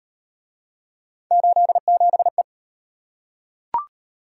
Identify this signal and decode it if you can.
Morse code, heard as a keyed tone, about 36 wpm, 700 Hz. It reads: 87E